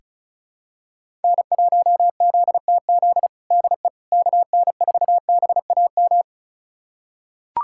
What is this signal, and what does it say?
Morse code, heard as a keyed tone, about 36 wpm, 700 Hz. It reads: N1ZTZ DE KN4BAM
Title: USA Callsign DE Callsign 35wpm